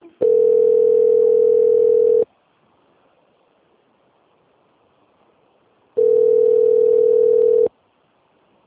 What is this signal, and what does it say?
Ringback tone (North America), 440 + 480 Hz.
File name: out- - - - - .15986